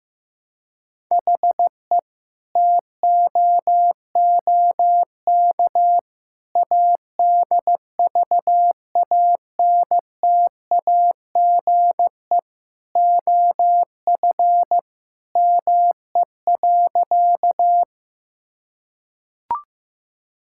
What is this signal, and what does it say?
Morse code, heard as a keyed tone, about 15 wpm, 700 Hz. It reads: HE TOOK ADVANTAGE OF ME.
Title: Sentences from Top 1000 Words - Review Part 6 - 15wpm